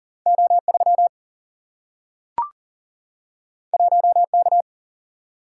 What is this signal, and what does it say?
Morse code, heard as a keyed tone, about 40 wpm, 700 Hz. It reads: O 3 1 K
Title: Single-Letter Single-Number 40wpm